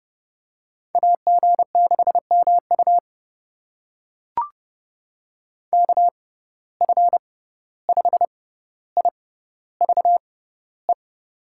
Morse code, heard as a keyed tone, about 29 wpm, spaced as for 14 wpm, 700 Hz. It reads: AG6MU KF5IVE
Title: USA Call Signs - 6x Character Spacing - 30wpm